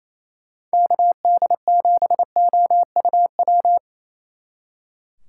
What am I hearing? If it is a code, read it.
Morse code, heard as a keyed tone, about 34 wpm, 700 Hz. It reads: KD7OUW
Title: Speed Racing - USA 2x3 Call Signs 28wpm